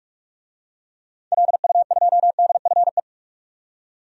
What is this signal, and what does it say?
Morse code, heard as a keyed tone, about 42 wpm, 700 Hz. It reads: LU2DFE
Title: DXCC-100 Call Signs - 6 Characters - 45wpm